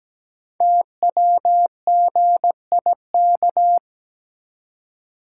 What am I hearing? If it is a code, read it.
Morse code, heard as a keyed tone, about 17 wpm, 700 Hz. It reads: TWGIK
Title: Code Groups - 5 Letters - 17wpm